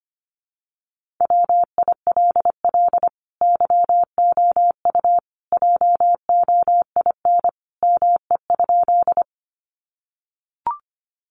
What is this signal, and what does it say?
Morse code, heard as a keyed tone, about 25 wpm, 700 Hz. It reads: WILL YOU JOIN ME?
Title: Sentences from Top 1400 Words 25wpm